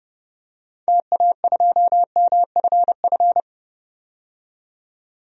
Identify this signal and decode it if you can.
Morse code, heard as a keyed tone, about 32 wpm, 700 Hz. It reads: TA2MFF
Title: DXCC-100 Call Signs - 6 Characters - 30wpm